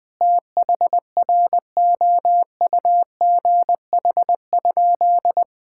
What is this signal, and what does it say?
Morse code, heard as a keyed tone, about 21 wpm, 700 Hz. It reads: THROUGH?
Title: Sentences from Top 1000 Words - Review Part 6 - 20wpm